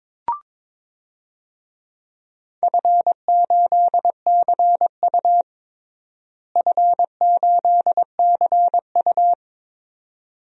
Morse code, heard as a keyed tone, about 21 wpm, 700 Hz. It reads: F8CU F8CU
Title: DXCC-100 Call Signs - 4 Characters - Repeated 2x - 22wpm